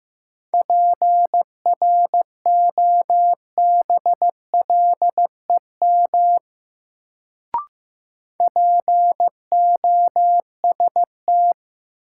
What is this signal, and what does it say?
Morse code, heard as a keyed tone, about 15 wpm, 700 Hz. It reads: PROBLEM POST
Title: Sentences from Top 1000 Words - New Word Drill - 15wpm